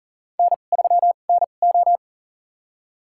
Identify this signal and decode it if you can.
Morse code, heard as a keyed tone, about 40 wpm, spaced as for 29 wpm, 700 Hz. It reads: N3NO
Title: USA Call Signs - 4 Characters - 2x Character Spacing - 40wpm